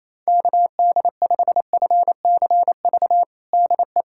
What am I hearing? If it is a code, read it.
Morse code, heard as a keyed tone, about 28 wpm, 700 Hz. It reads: KD5FCV DE
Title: USA Callsign DE Callsign 28wpm